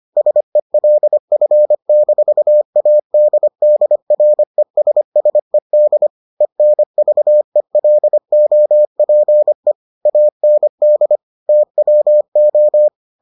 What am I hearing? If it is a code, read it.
Morse code, heard as a keyed tone, about 25 wpm, 600 Hz. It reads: SELF-ADDRESSED ENVELOPE AND TWO